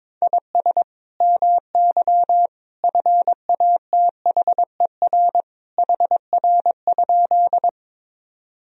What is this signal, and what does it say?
Morse code, heard as a keyed tone, about 22 wpm, 700 Hz. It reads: IS MY FATHER HR?
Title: Sentences from Top 300 Words and Abbreviations 22wpm